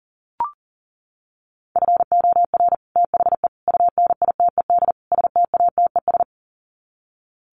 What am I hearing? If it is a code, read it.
Morse code, heard as a keyed tone, about 37 wpm, 700 Hz. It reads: FOR THE UNITED STATES
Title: Sets of 4 Words 40wpm